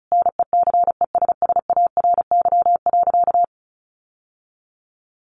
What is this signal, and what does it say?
Morse code, heard as a keyed tone, about 34 wpm, 700 Hz. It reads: NECESSARY.
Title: Sentences from Top 800 Words 35wpm